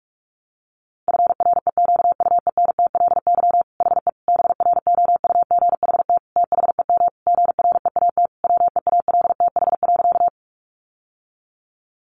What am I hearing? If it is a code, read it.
Morse code, heard as a keyed tone, about 44 wpm, 700 Hz. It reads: FREQUENTLY HE BROUGHT THEM GREAT WEALTH.
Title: Sentences from Top 1300 Words 45wpm